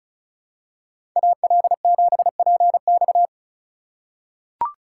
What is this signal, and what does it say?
Morse code, heard as a keyed tone, about 34 wpm, 700 Hz. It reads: AL7PX
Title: USA Call Signs - 5 Characters - 3x Character Spacing - 35wpm